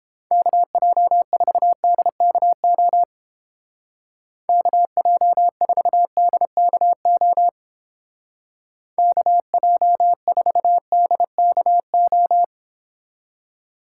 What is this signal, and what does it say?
Morse code, heard as a keyed tone, about 29 wpm, 700 Hz. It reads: KJ4DKO KJ4DKO KJ4DKO